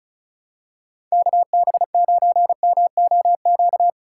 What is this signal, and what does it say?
Morse code, heard as a keyed tone, about 34 wpm, 700 Hz. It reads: KB9MOQ
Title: USA Call Signs - 3x Character Spacing - 35wpm